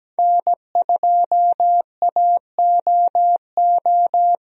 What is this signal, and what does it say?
Morse code, heard as a keyed tone, about 17 wpm, 700 Hz. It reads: N2AOO
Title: USA Call Signs - 5 Characters - Repeated 2x - 17wpm